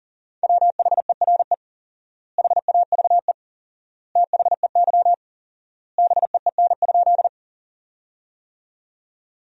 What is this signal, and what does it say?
Morse code, heard as a keyed tone, about 40 wpm, 700 Hz. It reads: WHERE HAVE THEY BEEN?
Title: 4x Word-Spacing - Sentences from Top 200 Words - 40wpm